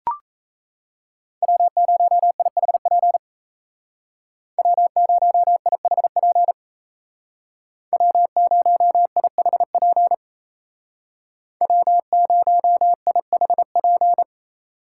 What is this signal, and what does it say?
Morse code, heard as a keyed tone, about 34 wpm, 700 Hz. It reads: W0IHP W0IHP W0IHP W0IHP
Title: Speed Racing - USA 1x3 Call Signs 28wpm